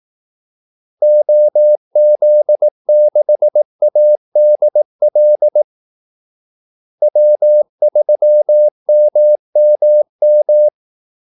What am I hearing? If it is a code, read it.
Morse code, heard as a keyed tone, about 18 wpm, 600 Hz. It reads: OZ6ADL W3MMM
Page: Kaldesignaler 18 wpm